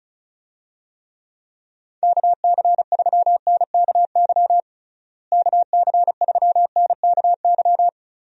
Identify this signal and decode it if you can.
Morse code, heard as a keyed tone, about 36 wpm, 700 Hz. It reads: KC3NKY KC3NKY
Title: USA Call Signs - 6 Characters - Repeated 3x - 35wpm